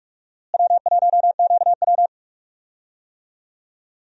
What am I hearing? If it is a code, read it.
Morse code, heard as a keyed tone, about 41 wpm, 700 Hz. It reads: W1QW
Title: USA Call Signs - 4 Characters - Repeated 2x - 45wpm